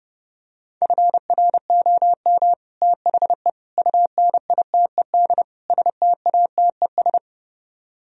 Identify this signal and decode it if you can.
Morse code, heard as a keyed tone, about 32 wpm, 700 Hz. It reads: FROM THE UNITED STATES